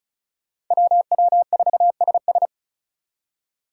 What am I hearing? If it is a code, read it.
Morse code, heard as a keyed tone, about 46 wpm, 700 Hz. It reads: WW4SS